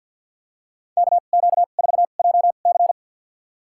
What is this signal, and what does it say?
Morse code, heard as a keyed tone, about 48 wpm, spaced as for 35 wpm, 700 Hz. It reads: KQ4JC
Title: USA Call Signs - 5 Characters - 2x Character Spacing - 50wpm